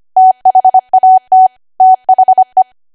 Morse code, heard as a keyed tone, about 25 wpm, 750 Hz. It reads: THAT THE
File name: Archive / 25 WPM / 25WPM